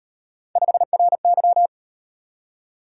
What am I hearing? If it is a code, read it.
Morse code, heard as a keyed tone, about 36 wpm, 700 Hz. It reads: 5RY